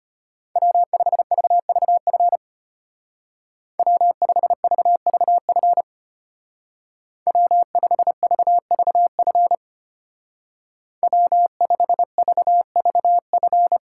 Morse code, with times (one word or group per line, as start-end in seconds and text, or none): 0.6-2.4 W5VVF
3.8-5.8 W5VVF
7.3-9.6 W5VVF
11.0-13.8 W5VVF